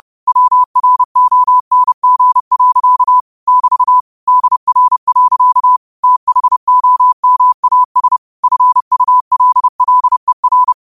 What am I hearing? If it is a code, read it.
Morse code, heard as a keyed tone, about 30 wpm, 1000 Hz. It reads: WRONG. <BT> DR. THOMAS FULLER
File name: qotdfiles / 30wpm